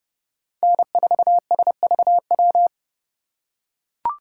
Morse code, heard as a keyed tone, about 29 wpm, 700 Hz. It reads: N4SVW